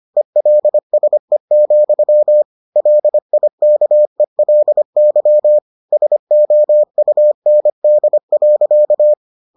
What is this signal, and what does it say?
Morse code, heard as a keyed tone, about 25 wpm, 600 Hz. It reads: ELSE, LIKELY SOUND.